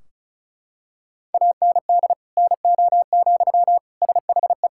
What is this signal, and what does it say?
Morse code, heard as a keyed tone, about 36 wpm, 700 Hz. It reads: AND NO, SHE
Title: Sentences from Top 1600 Words 35wpm